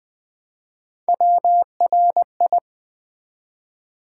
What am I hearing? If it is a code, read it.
Morse code, heard as a keyed tone, about 20 wpm, 700 Hz. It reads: WRI
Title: Build A Word - 1 Letter At A Time - 5-Letter Words - 20wpm